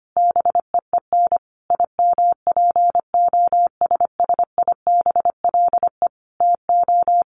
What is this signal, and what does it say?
Morse code, heard as a keyed tone, about 25 wpm, 700 Hz. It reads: BEEN IMPOSSIBLE TO
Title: Sentences from Top 1100 Words 25wpm